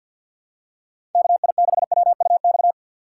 Morse code, heard as a keyed tone, about 47 wpm, 700 Hz. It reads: KI6WUX
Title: USA Call Signs - 4x Character Spacing - 50wpm